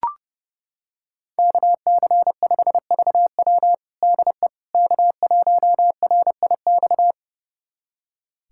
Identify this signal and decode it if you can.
Morse code, heard as a keyed tone, about 30 wpm, 700 Hz. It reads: KC5VW DE K1RIX
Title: USA Callsign DE Callsign 30wpm